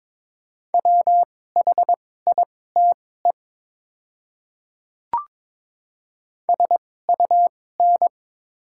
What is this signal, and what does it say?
Morse code, heard as a keyed tone, about 22 wpm, spaced as for 16 wpm, 700 Hz. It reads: WHITE SUN
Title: 2x Character Spacing - Top 500 Words - 22wpm